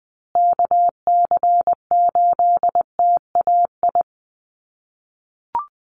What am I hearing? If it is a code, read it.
Morse code, heard as a keyed tone, about 20 wpm, 700 Hz. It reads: KC8TAI
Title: USA Call Signs Repeated 3x 20wpm